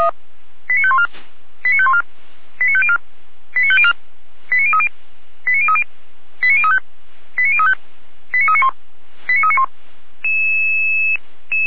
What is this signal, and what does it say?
Five-tone selective call (ZVEI), heard as quick runs of five tones, tones of about 70 ms. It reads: 89514 89514 89504 89504 89030 89030 89035 89035 89391 89391